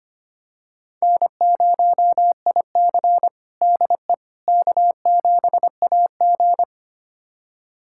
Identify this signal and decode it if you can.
Morse code, heard as a keyed tone, about 25 wpm, 700 Hz. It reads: N0IC DE K7AG
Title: USA Callsign DE Callsign - 4 Characters - 25wpm